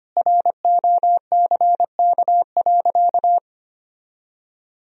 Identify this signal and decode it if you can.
Morse code, heard as a keyed tone, about 25 wpm, 700 Hz. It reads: ROCK.
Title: Sentences from Top 1000 Words - Review Part 2 - 25wpm